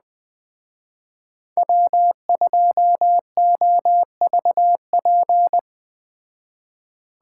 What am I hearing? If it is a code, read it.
Morse code, heard as a keyed tone, about 20 wpm, 700 Hz. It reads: W2OVP